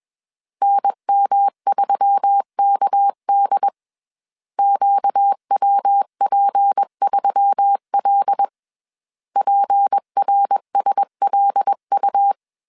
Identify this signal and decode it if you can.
Morse code, heard as a keyed tone, about 21 wpm, 800 Hz. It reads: NM3KD QWP3L PRSLU
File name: CW / DET / 1 6 1500C S A 20wpm 2048